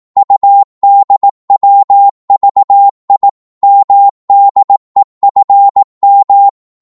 Morse code, heard as a keyed tone, about 18 wpm, 800 Hz. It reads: UDWVI MDEFM
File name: Morse / ltr-18wpm